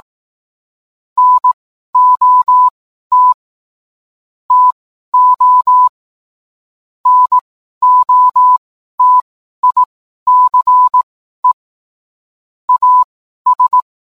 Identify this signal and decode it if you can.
Morse code, heard as a keyed tone, about 18 wpm, spaced as for 12 wpm, 1000 Hz. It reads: NOT TO NOTICE AS